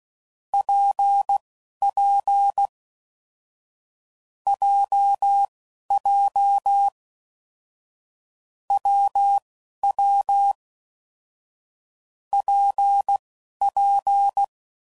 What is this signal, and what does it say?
Morse code, heard as a keyed tone, about 16 wpm, spaced as for 11 wpm, 800 Hz. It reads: PP JJ WW PP
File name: Combinations / Difficult Combos / WPJ WPJ1